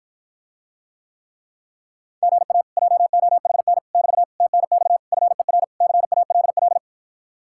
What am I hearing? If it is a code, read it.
Morse code, heard as a keyed tone, about 54 wpm, 700 Hz. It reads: GA JOHN <BT> TNX FER CALL